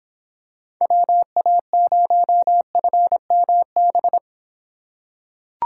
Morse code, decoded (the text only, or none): WA0FMB